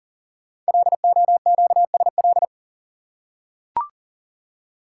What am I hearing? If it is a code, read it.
Morse code, heard as a keyed tone, about 40 wpm, 700 Hz. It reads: LOQSL
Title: Code Groups - 5 Letters - 40wpm